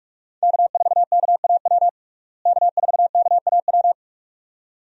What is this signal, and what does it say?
Morse code, heard as a keyed tone, about 46 wpm, 700 Hz. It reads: K4KAW K4KAW
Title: USA Call Signs - 5 Characters - Repeated 2x - 45wpm